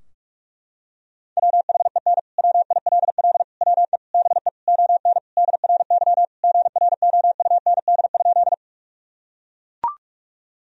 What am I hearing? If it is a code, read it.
Morse code, heard as a keyed tone, about 42 wpm, 700 Hz. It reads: WHEN WILL WE BE ON DRY GROUND?
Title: Sentences from Top 1000 Words - Review Part 3 - 45wpm